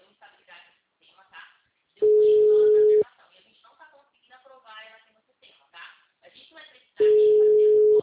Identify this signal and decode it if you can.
Ringback tone (Europe, 425 Hz), with speech.